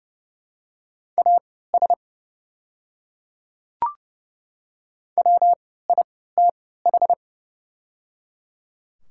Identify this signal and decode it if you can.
Morse code, heard as a keyed tone, about 30 wpm, spaced as for 17 wpm, 700 Hz. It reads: AS WITH
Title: 3x Character Spacing - Top 100 Words - 30wpm